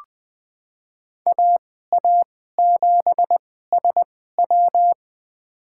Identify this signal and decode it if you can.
Morse code, heard as a keyed tone, about 20 wpm, spaced as for 14 wpm, 700 Hz. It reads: AA7SW
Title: USA Call Signs - 5 Characters - 2x Character Spacing - 20wpm